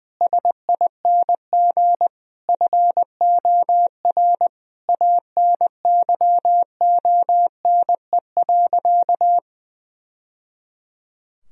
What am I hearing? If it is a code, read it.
Morse code, heard as a keyed tone, about 20 wpm, 700 Hz. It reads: SING FOR ANYONE.
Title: Sentences from Top 1000 Words - Review Part 2 - 20wpm